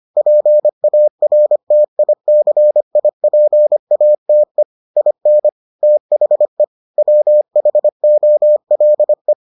Morse code, heard as a keyed tone, about 25 wpm, 600 Hz. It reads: PARTICIPATE IN THE WHOLE